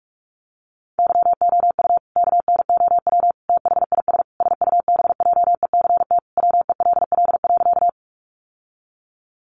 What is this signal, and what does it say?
Morse code, heard as a keyed tone, about 44 wpm, 700 Hz. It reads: YOU KNOW THIS SUBJECT WELL.
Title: Sentences from Top 800 Words 45wpm